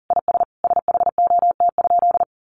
Morse code, heard as a keyed tone, about 40 wpm, 700 Hz. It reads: IS SHOT?
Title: Sentences from Top 1100 Words 40wpm